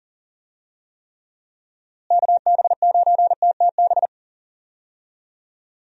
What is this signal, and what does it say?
Morse code, heard as a keyed tone, about 37 wpm, 700 Hz. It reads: KB9TTB
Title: USA Call Signs - 6 Characters - Repeated 2x - 40wpm